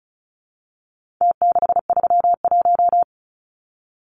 Morse code, heard as a keyed tone, about 34 wpm, 700 Hz. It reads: T631
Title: Numbers - Contest Style - 4-Digit 35wpm